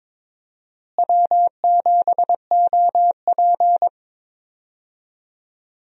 Morse code, heard as a keyed tone, about 22 wpm, 700 Hz. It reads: W7OP
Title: USA Call Signs - 4 Characters - Repeated 3x - 22wpm